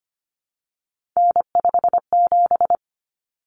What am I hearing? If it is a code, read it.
Morse code, heard as a keyed tone, about 25 wpm, 700 Hz. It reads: N57
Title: Common Cut Numbers - 3-Digit 25wpm